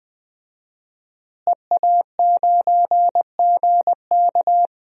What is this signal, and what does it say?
Morse code, heard as a keyed tone, about 20 wpm, 700 Hz. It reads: EA9GK
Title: DXCC-100 Call Signs - 5 Characters - Repeated 2x - 20wpm